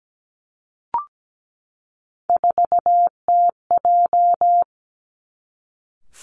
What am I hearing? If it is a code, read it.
Morse code, heard as a keyed tone, about 17 wpm, 700 Hz. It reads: 4TJ
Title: Three Challenging Letter-Number Combinations 17wpm